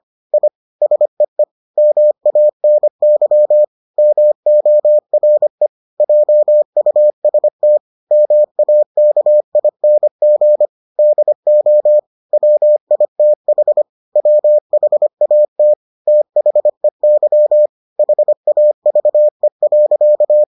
Morse code, heard as a keyed tone, about 25 wpm, 600 Hz. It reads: I SEE MANY MORE JUST MAKING DO WITH WHAT THEY HAVE.